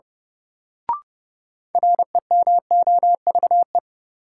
Morse code, heard as a keyed tone, about 30 wpm, 700 Hz. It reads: REMOVE
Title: Sentences from Top 800 Words - New Word Drill - 30wpm